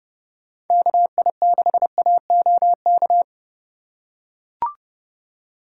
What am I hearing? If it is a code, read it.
Morse code, heard as a keyed tone, about 29 wpm, 700 Hz. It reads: KI6AOK